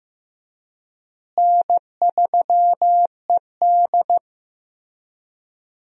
Morse code, heard as a keyed tone, about 15 wpm, 700 Hz. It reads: N3ED
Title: USA Call Signs - 4 Characters - Repeated 3x - 15wpm